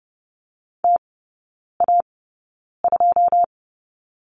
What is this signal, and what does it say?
Morse code, heard as a keyed tone, about 30 wpm, spaced as for 9 wpm, 700 Hz. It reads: TA2
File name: 081 Lesson 14e - Number 2 - Three Characters 30wpm